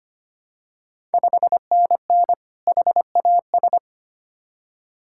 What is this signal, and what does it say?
Morse code, heard as a keyed tone, about 25 wpm, 700 Hz. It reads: 5NN HAS
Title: Ontario QSO Party Exchange 25wpm